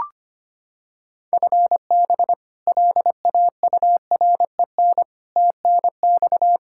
Morse code, heard as a keyed tone, about 25 wpm, 700 Hz. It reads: FB LAUREN TNX
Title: K1USN SST QSO Practice 25wpm